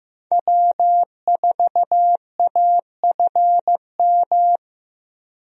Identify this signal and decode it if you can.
Morse code, heard as a keyed tone, about 15 wpm, 700 Hz. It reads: W4AFM
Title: USA Call Signs - 5 Characters - 15wpm